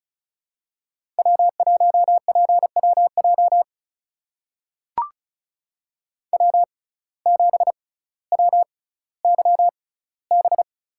Morse code, heard as a keyed tone, about 36 wpm, spaced as for 18 wpm, 700 Hz. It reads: W1PWJ W7WYB
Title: USA Call Signs - 5 Characters - 6x Character Spacing - 35wpm